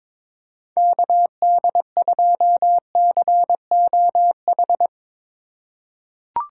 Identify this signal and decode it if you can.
Morse code, heard as a keyed tone, about 22 wpm, 700 Hz. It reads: KD2COH
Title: USA Call Signs - 3x Character Spacing - 22wpm